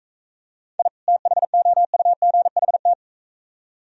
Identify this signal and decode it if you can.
Morse code, heard as a keyed tone, about 44 wpm, 690 Hz. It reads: I THOUGHT